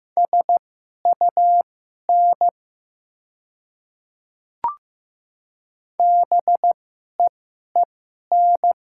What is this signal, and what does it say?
Morse code, heard as a keyed tone, about 15 wpm, spaced as for 11 wpm, 700 Hz. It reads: SUN BEEN
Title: 2x Character Spacing - Top 300 Words - 15wpm